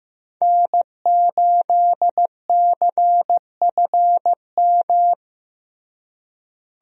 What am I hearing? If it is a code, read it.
Morse code, heard as a keyed tone, about 15 wpm, 700 Hz. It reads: N8CFM
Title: USA Call Signs - 5 Characters - Repeated 3x - 15wpm